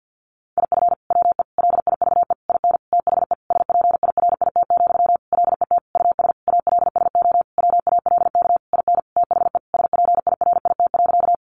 Morse code, heard as a keyed tone, about 50 wpm, 700 Hz. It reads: IF WE LIVE IN THE SPIRIT, LET US ALSO WALK IN THE SPIRIT.
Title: Sentences from Top 1400 Words 50wpm